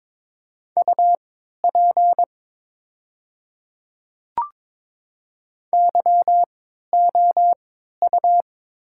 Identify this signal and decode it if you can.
Morse code, heard as a keyed tone, about 22 wpm, spaced as for 12 wpm, 700 Hz. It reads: UP YOU
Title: 3x Character Spacing - Top 100 Words - 22wpm